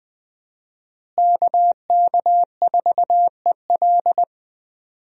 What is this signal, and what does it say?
Morse code, heard as a keyed tone, about 21 wpm, 700 Hz. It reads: KK4EL
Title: USA Call Signs - 5 Characters - Repeated 3x - 20wpm